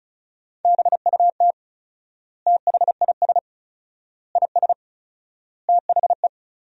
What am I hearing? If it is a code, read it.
Morse code, heard as a keyed tone, about 34 wpm, 700 Hz. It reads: BUT THIS IS THE
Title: 4x Word-Spacing - Sentences from Top 200 Words - 35wpm